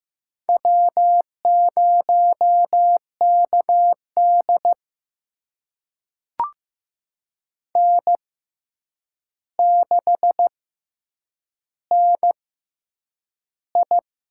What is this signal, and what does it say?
Morse code, heard as a keyed tone, about 15 wpm, spaced as for 8 wpm, 700 Hz. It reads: W0KD N6NI
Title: USA Call Signs - 4 Characters - 6x Character Spacing - 15wpm